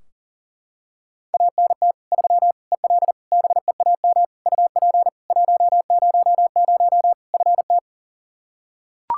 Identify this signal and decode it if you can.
Morse code, heard as a keyed tone, about 37 wpm, 700 Hz. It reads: ANT 3 EL BEAM UP 100 FT
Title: Practice QSO - Exchange 2 - Info - 40wpm